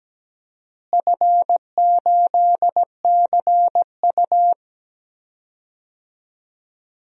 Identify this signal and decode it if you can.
Morse code, heard as a keyed tone, about 17 wpm, 700 Hz. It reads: F8CU
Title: DXCC-100 Call Signs - 4 Characters - Repeated 2x - 17wpm